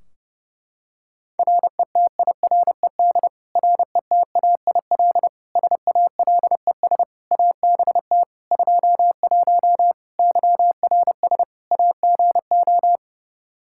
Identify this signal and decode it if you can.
Morse code, heard as a keyed tone, about 32 wpm, 700 Hz. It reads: RETIRED RETAIL SALES ABT 21 YRS AGO
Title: Practice QSO - Exchange 3 - More Info - 30wpm